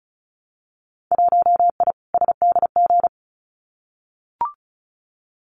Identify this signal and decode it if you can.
Morse code, heard as a keyed tone, about 35 wpm, 700 Hz. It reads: 1I SDG
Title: Winter Field Day Exchanges 35wpm